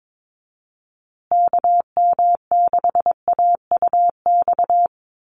Morse code, heard as a keyed tone, about 22 wpm, 700 Hz. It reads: KM6AUX
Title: USA Call Signs Repeated 2x 22wpm